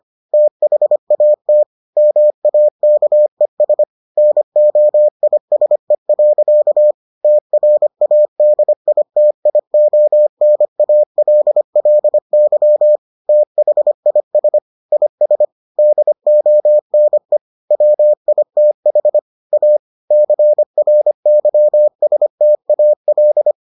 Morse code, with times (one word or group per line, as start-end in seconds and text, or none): 0.3-1.6 THAT
2.0-3.8 MAKES
4.2-6.9 NOISE.
7.2-13.0 TRADITIONALLY
13.3-14.6 THIS
14.9-15.4 IS
15.8-17.4 DONE
17.7-19.2 WITH
19.5-19.8 A
20.1-23.5 CRYSTAL